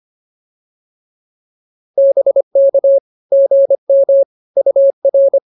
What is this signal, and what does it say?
Morse code, heard as a keyed tone, about 24 wpm, 550 Hz. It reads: BK GM UR